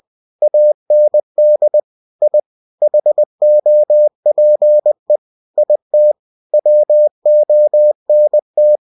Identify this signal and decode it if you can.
Morse code, heard as a keyed tone, about 20 wpm, 600 Hz. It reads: AND I HOPE IT WONT
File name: cw-books / winnie-the-pooh / winnie-20wpm-page-0019